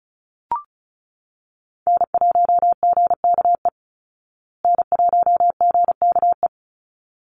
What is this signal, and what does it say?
Morse code, heard as a keyed tone, about 34 wpm, 700 Hz. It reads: N1GKE N1GKE
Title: Call Signs Repeated 3x - 35wpm